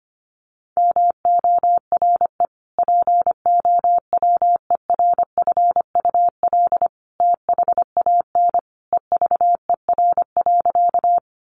Morse code, heard as a keyed tone, about 25 wpm, 700 Hz. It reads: MORE POWERFUL THAN EVER.
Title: Sentences from Top 1300 Words 25wpm